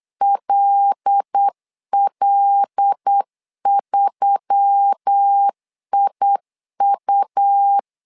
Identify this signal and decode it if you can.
Morse code, heard as a keyed tone, about 8 wpm, 800 Hz. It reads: LL3IU
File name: LEZIONI / 5 LBVC / 5 L 8wpm 2831